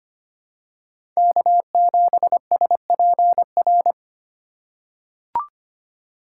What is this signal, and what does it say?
Morse code, heard as a keyed tone, about 25 wpm, 700 Hz. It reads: K7SPR
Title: USA Call Signs - 5 Characters - 4x Character Spacing - 25wpm